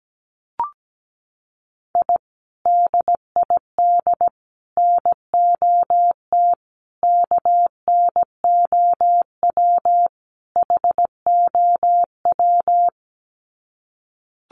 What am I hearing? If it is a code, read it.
Morse code, heard as a keyed tone, about 17 wpm, 700 Hz. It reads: I DID NOT KNOW HOW
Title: Sets of 5 Words Encore 17wpm